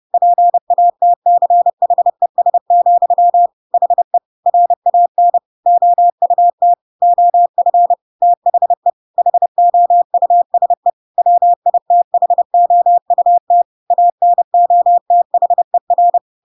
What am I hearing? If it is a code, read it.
Morse code, heard as a keyed tone, about 30 wpm, 700 Hz. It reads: PATCHES, HE RAN OUT OF THE HOUSE WITHOUT ANOTHER
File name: Pinocchio / Pinocchio ch8 30 30 ik1hge